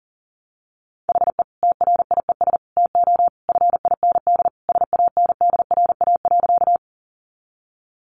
Morse code, heard as a keyed tone, about 40 wpm, 700 Hz. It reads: HE TRIES TO FIND SANDRA.
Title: Easy Phrases 40wpm